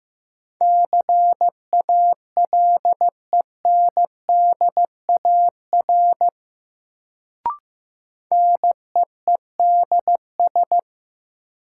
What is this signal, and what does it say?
Morse code, heard as a keyed tone, about 15 wpm, 700 Hz. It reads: CALENDAR NEEDS
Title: Sentences from Top 1600 Words - New Word Drill Alt - 15wpm